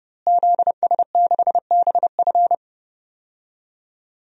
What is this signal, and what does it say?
Morse code, heard as a keyed tone, about 30 wpm, 700 Hz. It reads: ZS6BF
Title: DXCC-100 Call Signs - 5 Characters - Repeated 2x - 30wpm